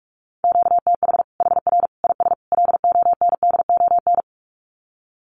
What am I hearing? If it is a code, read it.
Morse code, heard as a keyed tone, about 44 wpm, 700 Hz. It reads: QTH HR IS LONDON
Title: Common QSO Phrases 45wpm